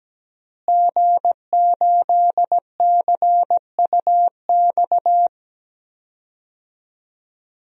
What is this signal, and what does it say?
Morse code, heard as a keyed tone, about 17 wpm, 700 Hz. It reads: G8CUX
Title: DXCC-100 Call Signs - 5 Characters - Repeated 2x - 17wpm